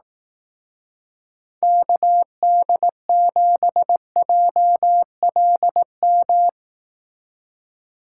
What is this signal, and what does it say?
Morse code, heard as a keyed tone, about 18 wpm, 700 Hz. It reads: KD7JLM
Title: Speed Racing - USA Call Signs 15wpm